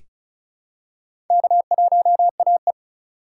Morse code, heard as a keyed tone, about 36 wpm, 690 Hz. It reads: K1AE
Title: USA Call Signs - 4 Characters - 3x Character Spacing - 35wpm